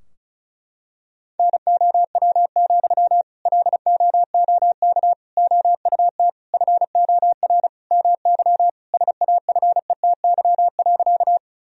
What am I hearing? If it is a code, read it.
Morse code, heard as a keyed tone, about 36 wpm, 700 Hz. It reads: NOW, LOOK OUT FOR MY SAFETY.
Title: Sentences from Top 1000 Words - Review Part 4 - 35wpm